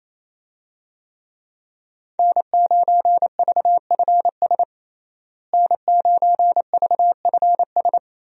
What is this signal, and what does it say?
Morse code, heard as a keyed tone, about 28 wpm, 700 Hz. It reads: N9VFS N9VFS
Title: USA Call Signs - 5 Characters - Repeated 2x - 28wpm